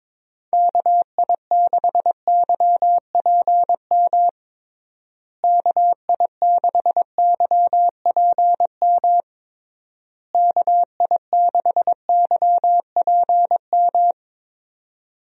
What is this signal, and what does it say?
Morse code, heard as a keyed tone, about 22 wpm, 700 Hz. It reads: KI6YPM KI6YPM KI6YPM
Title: USA Call Signs - 6 Characters - Repeated 3x - 22wpm